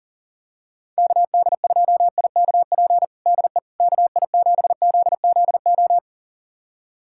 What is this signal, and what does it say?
Morse code, heard as a keyed tone, about 40 wpm, 700 Hz. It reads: KD2IKP DE KI7ZZO
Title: USA Callsign DE Callsign - 6 Characters - 40wpm